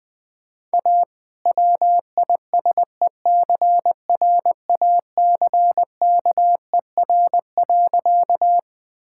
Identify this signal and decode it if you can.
Morse code, heard as a keyed tone, about 20 wpm, 700 Hz. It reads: A WISECRACKER.
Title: All-Age Appropriate Jokes - 20wpm